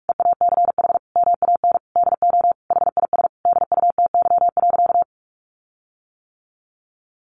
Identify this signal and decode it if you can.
Morse code, heard as a keyed tone, about 44 wpm, 700 Hz. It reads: EACH MAN DO HIS DUTY.
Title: Sentences from Top 1500 Words 45wpm